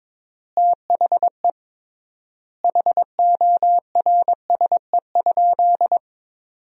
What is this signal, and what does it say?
Morse code, heard as a keyed tone, about 22 wpm, 700 Hz. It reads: THE HORSE?
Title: 3x Word-Spacing - Sentences from Top 500 Words - 22wpm